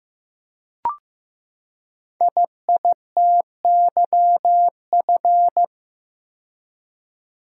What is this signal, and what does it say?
Morse code, heard as a keyed tone, about 15 wpm, 700 Hz. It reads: IITYF
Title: Code Groups - 5 Letters - 15wpm